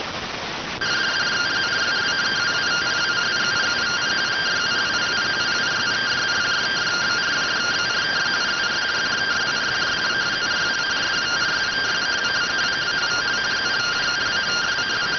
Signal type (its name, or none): two-tone teleprinter signal